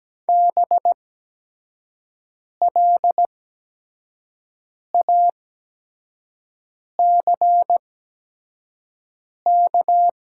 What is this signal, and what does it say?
Morse code, heard as a keyed tone, about 17 wpm, spaced as for 5 wpm, 700 Hz. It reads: BLACK
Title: 8x Character Spacing - Top 500 Words - 17wpm